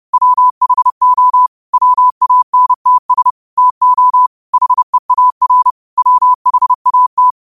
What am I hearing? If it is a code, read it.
Morse code, heard as a keyed tone, about 30 wpm, 1000 Hz. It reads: WHO WANTS TO HEAR WHAT